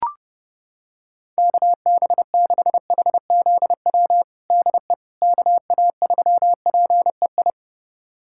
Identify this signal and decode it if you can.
Morse code, heard as a keyed tone, about 32 wpm, 700 Hz. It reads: KB6HZW DE KA3PEI
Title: USA Callsign DE Callsign - 6 Characters - 30wpm